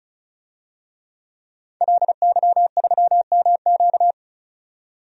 Morse code, heard as a keyed tone, about 36 wpm, 700 Hz. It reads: LY3MQ
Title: DXCC-100 Call Signs - 5 Characters - 35wpm